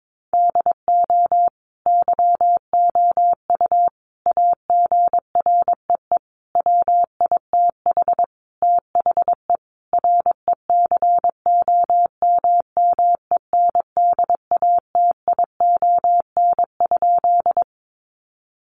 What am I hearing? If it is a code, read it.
Morse code, heard as a keyed tone, about 22 wpm, 700 Hz. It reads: DO YOU AGREE WITH THE RECOMMENDATION?
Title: Sentences from Top 1400 Words 22wpm